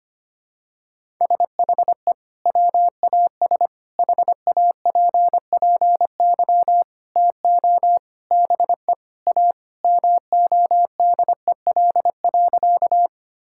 Morse code, heard as a keyed tone, about 25 wpm, 700 Hz. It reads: SHE WAS HAPPY TO BE A MODEL.
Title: Sentences from Top 1000 Words - Review Part 5 - 25wpm